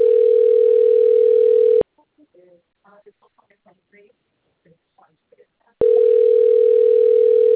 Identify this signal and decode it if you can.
Ringback tone (North America), 440 + 480 Hz, with speech.
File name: rg-1600-unknown- - - .8219